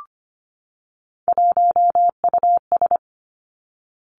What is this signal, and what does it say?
Morse code, heard as a keyed tone, about 25 wpm, 700 Hz. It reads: 1US
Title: Three Challenging Letter-Number Combinations 25wpm